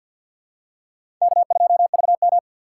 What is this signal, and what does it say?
Morse code, heard as a keyed tone, about 49 wpm, 700 Hz. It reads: K2VM
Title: USA Call Signs - 4 Characters - 6x Character Spacing - 50wpm